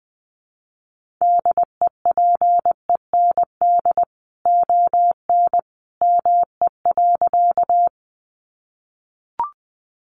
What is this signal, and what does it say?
Morse code, heard as a keyed tone, about 20 wpm, 700 Hz. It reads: DEPEND ON ME.
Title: Sentences from Top 1300 Words 20wpm